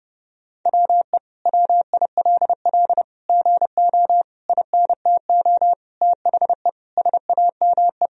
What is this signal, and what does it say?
Morse code, heard as a keyed tone, about 30 wpm, 700 Hz. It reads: WE WILL GO INTO THE SAME
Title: Sentences from Top 1000 Words - Review Part 4 - 30wpm